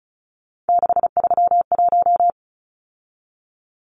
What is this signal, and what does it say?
Morse code, heard as a keyed tone, about 34 wpm, 700 Hz. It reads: 631
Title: Numbers - 3-Digit 35wpm